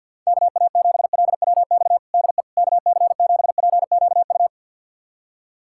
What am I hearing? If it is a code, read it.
Morse code, heard as a keyed tone, about 51 wpm, 700 Hz. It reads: KA7LWX DE KC7PQU
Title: USA Callsign DE Callsign - 6 Characters - 50wpm